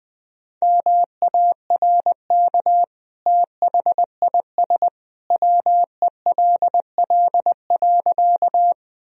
Morse code, heard as a keyed tone, about 21 wpm, 700 Hz. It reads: MARK THIS WELL.
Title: Sentences from Top 1000 Words - Review Part 1 - 20wpm